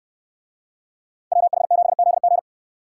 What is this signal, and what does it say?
Morse code, heard as a keyed tone, about 62 wpm, 700 Hz. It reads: WH7ZG